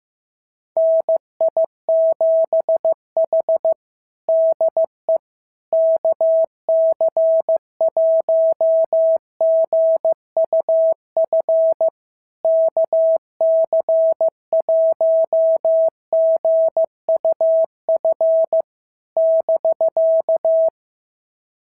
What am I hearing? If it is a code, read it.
Morse code, heard as a keyed tone, about 14 wpm, 650 Hz. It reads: NI7H DE KC1GUF KC1GUF <BK>
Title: SOTA QSO Practice 15wpm